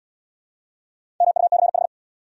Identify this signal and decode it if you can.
Morse code, heard as a keyed tone, about 66 wpm, 700 Hz. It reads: MUCH